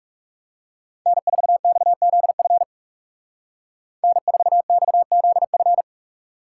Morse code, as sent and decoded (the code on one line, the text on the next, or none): -. ....- -..- --.. ..-. / -. ....- -..- --.. ..-.
N4XZF N4XZF